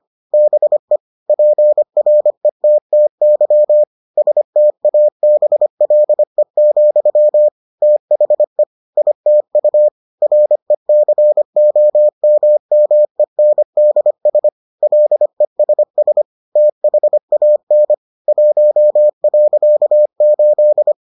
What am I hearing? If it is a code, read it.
Morse code, heard as a keyed tone, about 25 wpm, 600 Hz. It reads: BE PRETTY STABLE, THE ITU RECOMMENDS LESS THAN 1.8